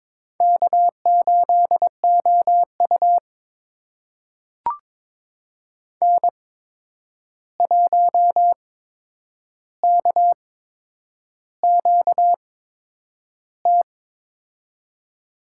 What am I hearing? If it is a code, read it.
Morse code, heard as a keyed tone, about 22 wpm, spaced as for 9 wpm, 700 Hz. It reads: K8OU N1KQT
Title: USA Call Signs - 8x Character Spacing - 22wpm